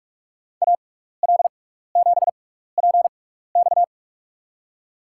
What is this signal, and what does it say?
Morse code, heard as a keyed tone, about 44 wpm, spaced as for 15 wpm, 700 Hz. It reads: AL7PX
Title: USA Call Signs - 5 Characters - 6x Character Spacing - 45wpm